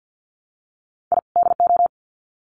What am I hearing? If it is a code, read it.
Morse code, heard as a keyed tone, about 50 wpm, 700 Hz. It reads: I DO
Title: Sets of 2 Words 50wpm